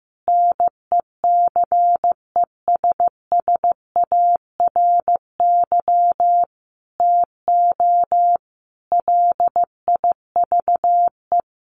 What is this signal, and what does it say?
Morse code, heard as a keyed tone, about 15 wpm, 700 Hz. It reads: NECESSARY TO LIVE
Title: Sentences from Top 800 Words 15wpm